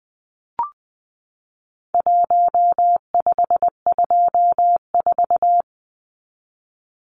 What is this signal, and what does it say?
Morse code, heard as a keyed tone, about 20 wpm, 700 Hz. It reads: 1524
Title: Numbers - 4-Digit 20wpm